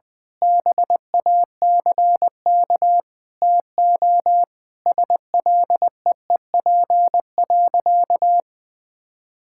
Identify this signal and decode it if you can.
Morse code, heard as a keyed tone, about 21 wpm, 700 Hz. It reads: BACK TO SLEEP.
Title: Sentences with 7-Word Phrases - 20wpm